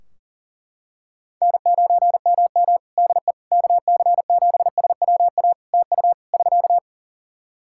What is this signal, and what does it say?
Morse code, heard as a keyed tone, about 40 wpm, 700 Hz. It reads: N9MM DE KC7SWA TU <SK>
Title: Practice QSO - Exchange 4 - Ending - 40wpm